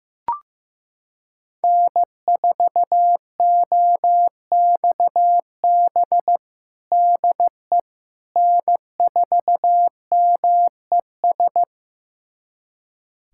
Morse code, heard as a keyed tone, about 16 wpm, 700 Hz. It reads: N4OXB DE N4MES
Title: USA Callsign DE Callsign - 5 Characters - 15wpm